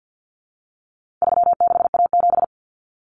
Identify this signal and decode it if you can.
Morse code, heard as a keyed tone, about 50 wpm, 700 Hz. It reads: 36A7
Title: Common Cut Numbers - 4-Digit 50wpm